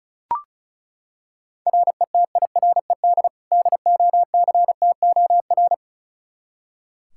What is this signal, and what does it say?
Morse code, heard as a keyed tone, about 36 wpm, 700 Hz. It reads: RETIRED DOCTOR
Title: Practice QSO - Exchange 3 - More Info - 35wpm